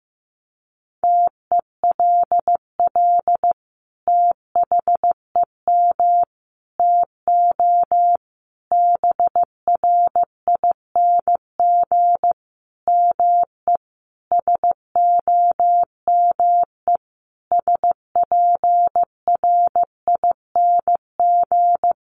Morse code, heard as a keyed tone, about 15 wpm, 700 Hz. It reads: TELL THEM TO BRING ME SOME SPRING
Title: Sentences from Top 1200 Words 15wpm